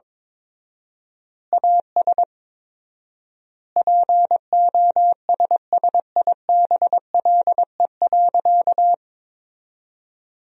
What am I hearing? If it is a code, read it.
Morse code, heard as a keyed tone, about 21 wpm, 700 Hz. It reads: AS POSSIBLE.
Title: 4x Word-Spacing - Sentences from Top 500 Words - 22wpm